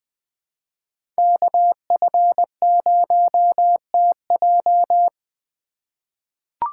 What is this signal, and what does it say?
Morse code, heard as a keyed tone, about 20 wpm, 700 Hz. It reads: KF0TJ
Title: USA Call Signs - 5 Characters - 3x Character Spacing - 20wpm